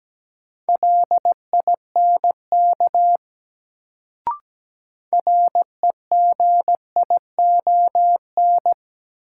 Morse code, heard as a keyed tone, about 17 wpm, 700 Hz. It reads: LINK REGION
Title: Sentences from Top 1500 Words - New Word Drill Alt - 17wpm